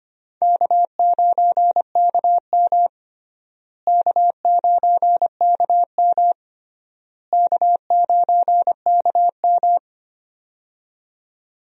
Morse code, heard as a keyed tone, about 25 wpm, 700 Hz. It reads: K9KM K9KM K9KM
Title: USA Call Signs - 4 Characters - Repeated 3x - 25wpm